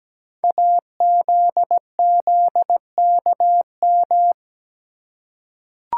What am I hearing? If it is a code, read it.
Morse code, heard as a keyed tone, about 17 wpm, 700 Hz. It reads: AZZKM
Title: Code Groups - 5 Letters - 17wpm